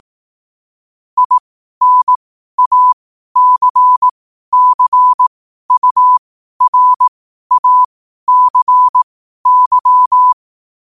Morse code, heard as a keyed tone, about 18 wpm, spaced as for 12 wpm, 1000 Hz. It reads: INACCURACY
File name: qotdfiles / 12wpm